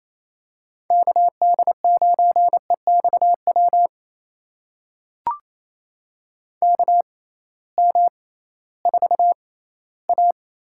Morse code, heard as a keyed tone, about 28 wpm, spaced as for 16 wpm, 700 Hz. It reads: KD9EXW KM4A
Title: USA Call Signs - 6 Characters - 6x Character Spacing - 28wpm